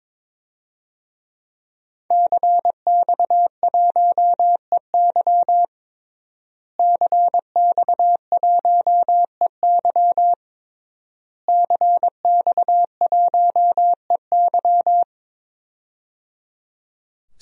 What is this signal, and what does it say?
Morse code, heard as a keyed tone, about 22 wpm, 700 Hz. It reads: CX1EY CX1EY CX1EY
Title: DXCC-100 Call Signs - 5 Characters - Repeated 3x - 22wpm